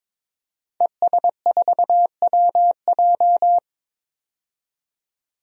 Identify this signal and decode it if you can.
Morse code, heard as a keyed tone, about 22 wpm, 700 Hz. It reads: ES4WJ
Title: DXCC-100 Call Signs - 5 Characters - 22wpm